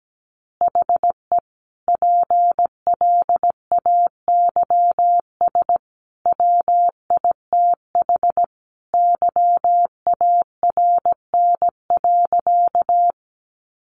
Morse code, heard as a keyed tone, about 17 wpm, 700 Hz. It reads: HE PLAYS WITH YARN.
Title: Easy Phrases 17wpm